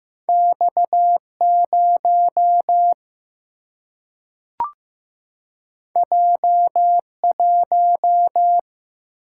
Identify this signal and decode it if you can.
Morse code, heard as a keyed tone, about 15 wpm, 700 Hz. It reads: X 0 J 1
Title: Single-Letter Single-Number 15wpm